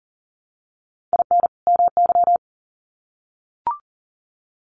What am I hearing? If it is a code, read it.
Morse code, heard as a keyed tone, about 40 wpm, 700 Hz. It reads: IN MY
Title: Sets of 2 Words 40wpm